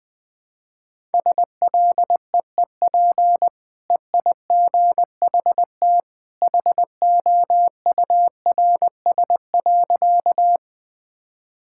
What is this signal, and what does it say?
Morse code, heard as a keyed tone, about 20 wpm, 700 Hz. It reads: SLEEP EIGHT HOURS.
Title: Easy Sentences with Hints - 20wpm